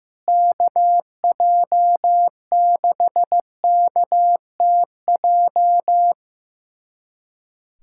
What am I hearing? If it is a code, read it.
Morse code, heard as a keyed tone, about 15 wpm, 700 Hz. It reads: KJ6KTJ
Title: USA Callsign DE Callsign 15wpm